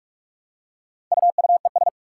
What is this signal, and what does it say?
Morse code, heard as a keyed tone, about 44 wpm, 700 Hz. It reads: UUES
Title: Four Challenging Letter-Number Combinations 45wpm